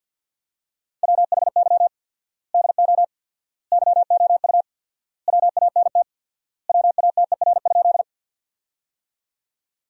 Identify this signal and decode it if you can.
Morse code, heard as a keyed tone, about 50 wpm, 700 Hz. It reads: WHY DO YOU WANT WATER?
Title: 4x Word-Spacing - Sentences from Top 200 Words - 50wpm